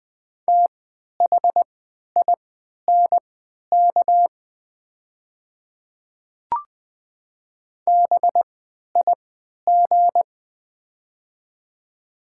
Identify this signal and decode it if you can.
Morse code, heard as a keyed tone, about 20 wpm, spaced as for 11 wpm, 700 Hz. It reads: THINK BIG
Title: 3x Character Spacing - Top 500 Words - 20wpm